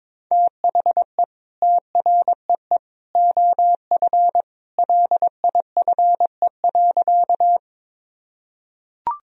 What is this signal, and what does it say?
Morse code, heard as a keyed tone, about 22 wpm, 700 Hz. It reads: THE TREE OF LIFE.
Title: Sentences from Top 1000 Words - Review Part 1 - 22wpm